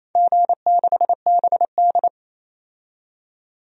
Morse code, heard as a keyed tone, about 27 wpm, 700 Hz. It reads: G6BD
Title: DXCC-100 Call Signs - 4 Characters - 28wpm